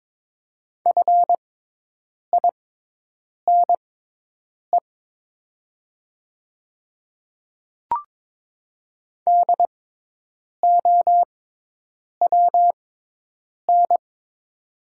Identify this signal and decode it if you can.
Morse code, heard as a keyed tone, about 22 wpm, spaced as for 8 wpm, 700 Hz. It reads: FINE DOWN
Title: 6x Character Spacing - Top 500 Words - 22wpm